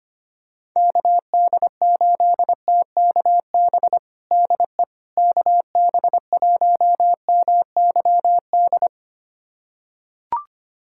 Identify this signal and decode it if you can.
Morse code, heard as a keyed tone, about 25 wpm, 700 Hz. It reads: KD8TKB DE KB1MYD
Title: USA Callsign DE Callsign - 6 Characters - 25wpm